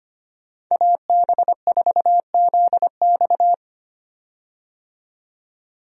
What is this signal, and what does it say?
Morse code, heard as a keyed tone, about 25 wpm, 700 Hz. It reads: AB4ZX
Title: USA Call Signs - 5 Characters - 25wpm